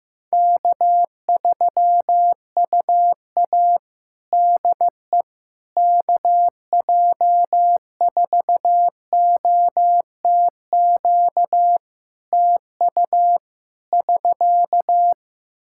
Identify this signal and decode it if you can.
Morse code, heard as a keyed tone, about 15 wpm, 700 Hz. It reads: K3UA DE KJ4OTQ TU <SK>
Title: Practice QSO - Exchange 4 - Ending - 15wpm